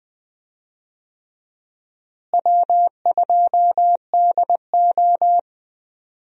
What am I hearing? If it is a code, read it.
Morse code, heard as a keyed tone, about 20 wpm, 700 Hz. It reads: W2DO
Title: USA Call Signs - 4 Characters - Repeated 3x - 20wpm